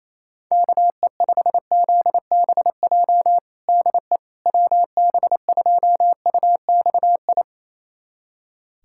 Morse code, heard as a keyed tone, about 28 wpm, 700 Hz. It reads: KE5ZBJ DE WB2UXI
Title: USA Callsign DE Callsign 28wpm